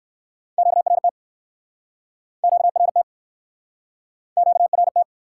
Morse code, heard as a keyed tone, about 60 wpm, 700 Hz. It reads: QRT QRT QRT